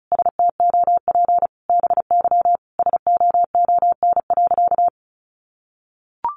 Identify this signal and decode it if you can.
Morse code, heard as a keyed tone, about 34 wpm, 700 Hz. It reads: STOP BY SOON.
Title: Sentences from Top 1500 Words 35wpm